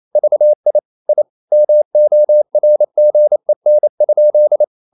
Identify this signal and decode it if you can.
Morse code, heard as a keyed tone, about 28 wpm, 600 Hz. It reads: VI I MORGEN?
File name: QRQ 28wpm